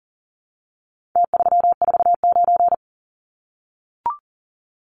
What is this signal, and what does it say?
Morse code, heard as a keyed tone, about 40 wpm, 700 Hz. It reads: T349
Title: Numbers - Contest Style - 4-Digit 40wpm